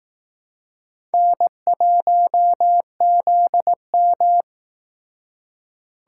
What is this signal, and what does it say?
Morse code, heard as a keyed tone, about 19 wpm, 700 Hz. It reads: N1ZM
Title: Speed Racing - USA Call Signs 15wpm